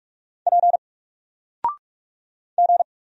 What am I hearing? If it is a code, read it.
Morse code, heard as a keyed tone, about 44 wpm, 700 Hz. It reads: P G
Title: Single Character (Letter:Number:Punctuation:Prosign) 45wpm